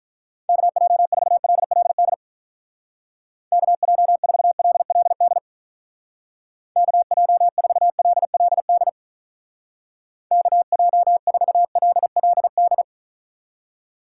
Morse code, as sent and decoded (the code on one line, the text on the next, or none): -.- .--- ....- .-.. .-.. -.. / -.- .--- ....- .-.. .-.. -.. / -.- .--- ....- .-.. .-.. -.. / -.- .--- ....- .-.. .-.. -..
KJ4LLD KJ4LLD KJ4LLD KJ4LLD